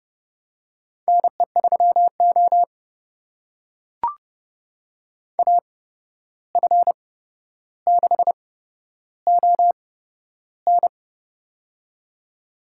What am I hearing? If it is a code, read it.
Morse code, heard as a keyed tone, about 31 wpm, spaced as for 12 wpm, 700 Hz. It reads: NE3O AF6ON
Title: USA Call Signs - 8x Character Spacing - 30wpm